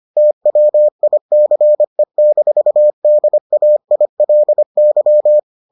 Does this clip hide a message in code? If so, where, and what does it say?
Yes: Morse code from 0.2 to 5.4 s, heard as a keyed tone, about 25 wpm, 600 Hz. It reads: TWICE-DAILY